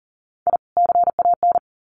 Morse code, heard as a keyed tone, about 40 wpm, 700 Hz. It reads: I CAN
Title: Sets of 2 Words 40wpm